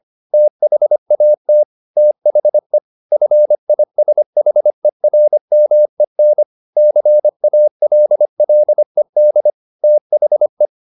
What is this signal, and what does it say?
Morse code, heard as a keyed tone, about 25 wpm, 600 Hz. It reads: THAT THE FISHERMEN CALLED THE
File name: the old man and the sea part 1 25wpm